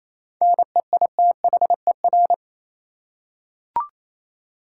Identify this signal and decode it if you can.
Morse code, heard as a keyed tone, about 28 wpm, 700 Hz. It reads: NEITHER
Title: Seven Letter Words - 28wpm